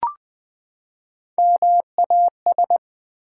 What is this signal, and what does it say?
Morse code, heard as a keyed tone, about 20 wpm, 700 Hz. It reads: MAS
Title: Build A Word - 3 Letters At A Time - 6-Letter Words - 20wpm